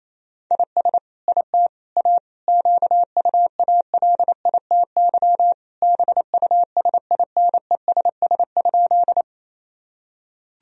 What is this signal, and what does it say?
Morse code, heard as a keyed tone, about 28 wpm, 700 Hz. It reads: IS IT A QUALITY BUSINESS?
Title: Sentences from Top 1000 Words - Review Part 4 - 28wpm